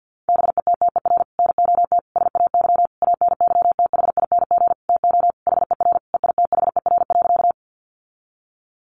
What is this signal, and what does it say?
Morse code, heard as a keyed tone, about 50 wpm, 700 Hz. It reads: BETTER NOT SAY ANYTHING TO HER EITHER.
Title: Sentences from Top 1000 Words 50wpm